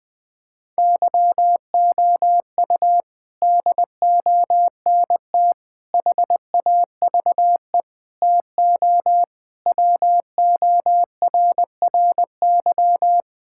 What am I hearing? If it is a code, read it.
Morse code, heard as a keyed tone, about 20 wpm, 700 Hz. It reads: YOU DONT HAVE TO WORRY
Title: Sets of 6 Words - 20wpm